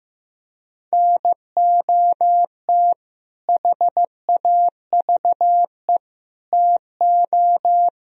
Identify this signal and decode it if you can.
Morse code, heard as a keyed tone, about 16 wpm, 700 Hz. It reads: NOT HAVE TO
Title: Speed-Racing - Sets of 4 Words 15wpm